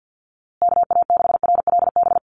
Morse code, heard as a keyed tone, about 50 wpm, 700 Hz. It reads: KA6RLB
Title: USA Call Signs Repeated 3x 50wpm